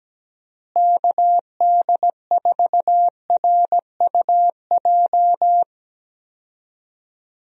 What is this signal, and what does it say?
Morse code, heard as a keyed tone, about 17 wpm, 700 Hz. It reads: KD4RUJ
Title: USA Call Signs - 6 Characters - Repeated 2x - 17wpm